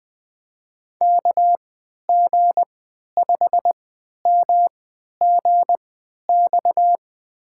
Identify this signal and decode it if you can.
Morse code, heard as a keyed tone, about 20 wpm, spaced as for 11 wpm, 700 Hz. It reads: KG5MGX
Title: USA Call Signs - 6 Characters - 3x Character Spacing - 20wpm